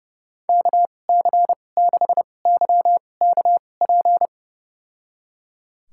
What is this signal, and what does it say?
Morse code, heard as a keyed tone, about 30 wpm, spaced as for 22 wpm, 700 Hz. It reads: KC6YKP
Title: USA Call Signs - 2x Character Spacing - 30wpm